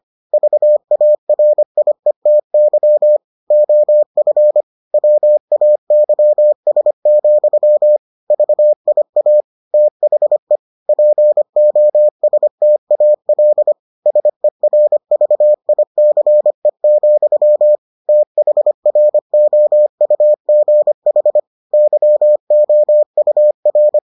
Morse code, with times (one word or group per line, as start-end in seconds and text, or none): 0.3-3.2 VARIETY
3.5-4.6 OF
4.9-8.0 WAYS,
8.3-9.4 VIA
9.7-10.6 THE
10.9-13.7 POSTAL
14.1-17.8 SERVICE,
18.1-21.4 THROUGH
21.7-24.0 YOUR